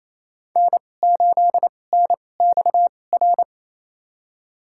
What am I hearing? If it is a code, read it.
Morse code, heard as a keyed tone, about 28 wpm, spaced as for 20 wpm, 700 Hz. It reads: N8NXR
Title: USA Call Signs - 5 Characters - 2x Character Spacing - 28wpm